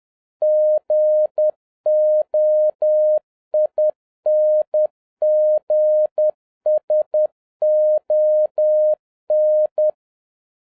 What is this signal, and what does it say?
Morse code, heard as a keyed tone, about 10 wpm, 610 Hz. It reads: GOINGS ON
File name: 2011-07-mp3 / n36 THE 3.11 2 10wpm